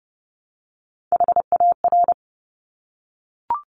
Morse code, heard as a keyed tone, about 30 wpm, 700 Hz. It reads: HAR
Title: Top Three Letter Combinations in Words 30wpm